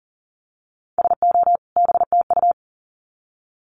Morse code, heard as a keyed tone, about 40 wpm, 700 Hz. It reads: SO BTU
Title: Common QSO Phrases 40wpm